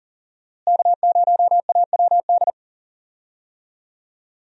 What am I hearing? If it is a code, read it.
Morse code, heard as a keyed tone, about 37 wpm, 700 Hz. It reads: K0AWD
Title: USA Call Signs - 5 Characters - Repeated 2x - 40wpm